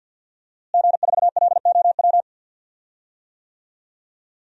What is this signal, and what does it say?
Morse code, heard as a keyed tone, about 51 wpm, 700 Hz. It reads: G4LOW
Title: DXCC-100 Call Signs - 5 Characters - Repeated 2x - 50wpm